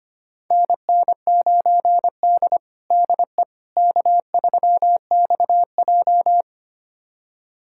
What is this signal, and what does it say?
Morse code, heard as a keyed tone, about 25 wpm, 700 Hz. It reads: NN9D DE K3XJ
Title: USA Callsign DE Callsign - 4 Characters - 25wpm